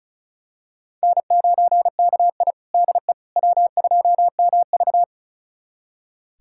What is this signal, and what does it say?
Morse code, heard as a keyed tone, about 34 wpm, 700 Hz. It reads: N9KI DE W2MV
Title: USA Callsign DE Callsign - 4 Characters - 35wpm